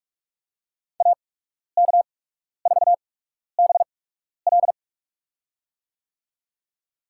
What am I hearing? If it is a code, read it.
Morse code, heard as a keyed tone, about 46 wpm, spaced as for 12 wpm, 700 Hz. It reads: AK4BL
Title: USA Call Signs - 5 Characters - 8x Character Spacing - 45wpm